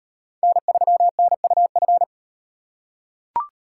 Morse code, heard as a keyed tone, about 37 wpm, 700 Hz. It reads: N3NUF